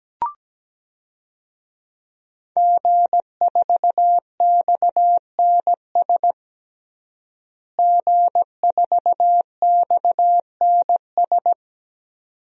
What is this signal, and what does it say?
Morse code, heard as a keyed tone, about 17 wpm, 700 Hz. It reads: G4XNS G4XNS
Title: DXCC-100 Call Signs - 5 Characters - Repeated 2x - 17wpm